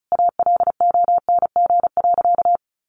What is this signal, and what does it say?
Morse code, heard as a keyed tone, about 34 wpm, 700 Hz. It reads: ALONG.
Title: Sentences from Top 1500 Words 35wpm